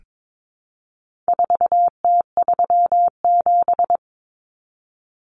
Morse code, heard as a keyed tone, about 22 wpm, 700 Hz. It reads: 4T37
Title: Common Cut Numbers - 4-Digit 22wpm